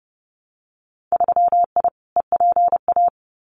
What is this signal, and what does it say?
Morse code, heard as a keyed tone, about 30 wpm, 700 Hz. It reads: 3I EPA
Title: Winter Field Day Exchanges Encore 30wpm